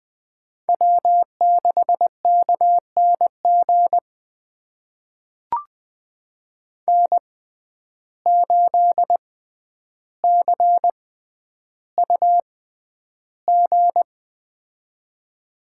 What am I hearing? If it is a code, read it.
Morse code, heard as a keyed tone, about 20 wpm, spaced as for 10 wpm, 700 Hz. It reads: W6KNG N8CUG
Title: USA Call Signs - 5 Characters - 6x Character Spacing - 20wpm